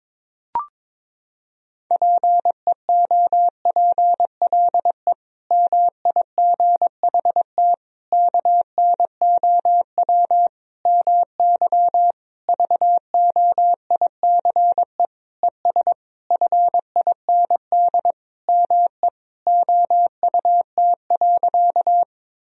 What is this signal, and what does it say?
Morse code, heard as a keyed tone, about 22 wpm, 700 Hz. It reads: PEOPLE MIGHT KNOW MY VOICE ES FIND ME OUT.
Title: Sentences from Top 500 Words and Abbreviations 22wpm